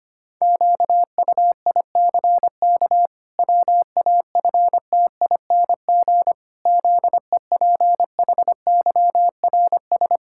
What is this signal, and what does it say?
Morse code, heard as a keyed tone, about 25 wpm, 700 Hz. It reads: QUICK WAFTING ZEPHYRS
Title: Pangram Challenge v2 - 25wpm